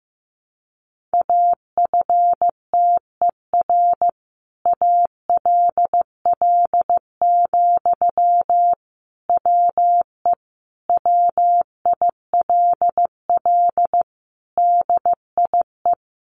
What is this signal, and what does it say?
Morse code, heard as a keyed tone, about 15 wpm, 700 Hz. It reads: AFTER ALL, WE WILL DIE
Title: Sentences from Top 1300 Words 15wpm